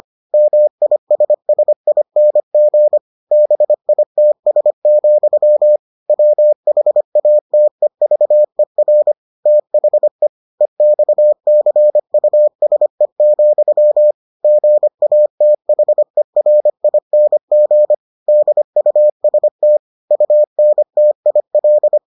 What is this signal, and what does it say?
Morse code, heard as a keyed tone, about 25 wpm, 600 Hz. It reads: MISSING BITS, WHATEVER THE EXCUSE, GATHERING DUST UNTIL